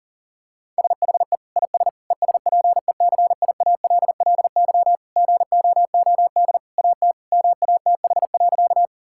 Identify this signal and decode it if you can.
Morse code, heard as a keyed tone, about 37 wpm, 700 Hz. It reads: SHE IS ESPECIALLY GOOD AT MATH.
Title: Sentences from Top 1000 Words - Review Part 6 - 40wpm